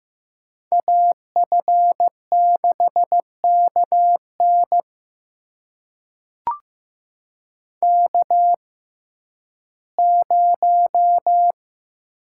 Morse code, heard as a keyed tone, about 15 wpm, spaced as for 11 wpm, 700 Hz. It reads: AF6KN K0
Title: USA Call Signs - 6x Character Spacing - 15wpm